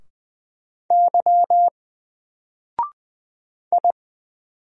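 Morse code, heard as a keyed tone, about 20 wpm, 700 Hz. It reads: Y I
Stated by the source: Single letter and number Morse drill at 20 wpm, from N4LCD.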